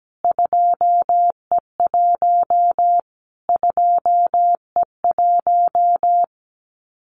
Morse code, heard as a keyed tone, about 17 wpm, 700 Hz. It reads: 2E1 2E1
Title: Callsign Prefixes for Top 25 DXCC - Repeated 3x - 17wpm